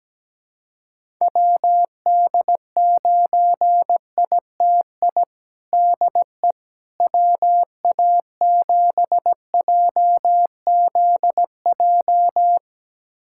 Morse code, heard as a keyed tone, about 17 wpm, 700 Hz. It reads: WD9ITI DE WA7JZJ
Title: USA Callsign DE Callsign - 6 Characters - 17wpm